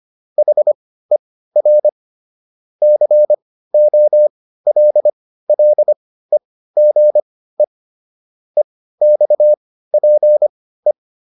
Morse code, heard as a keyed tone, about 25 wpm, spaced as for 15 wpm, 600 Hz. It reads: HER COLLEGE EXPE